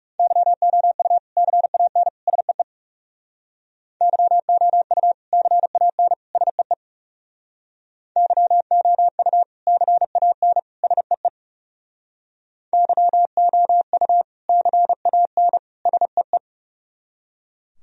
Morse code, heard as a keyed tone, about 37 wpm, 700 Hz. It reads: YOU CAN SEE YOU CAN SEE YOU CAN SEE YOU CAN SEE
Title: Speed-Racing - Sets of 3 Words 30wpm